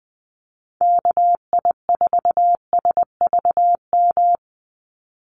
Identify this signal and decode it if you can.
Morse code, heard as a keyed tone, about 20 wpm, 700 Hz. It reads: KI4SVM
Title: NA SOTA Activator Callsigns 20wpm